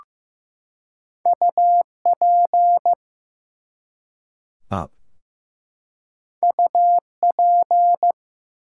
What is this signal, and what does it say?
Morse code, heard as a keyed tone, about 15 wpm, 700 Hz. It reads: UP UP
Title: Two Letter Words - 15wpm